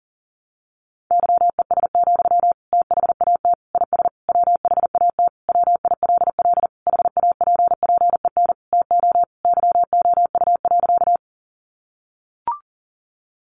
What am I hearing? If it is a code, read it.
Morse code, heard as a keyed tone, about 40 wpm, 700 Hz. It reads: YES, THAT IS WHAT WILL HAPPEN TO YOU.
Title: Sentences from Top 500 Words 40wpm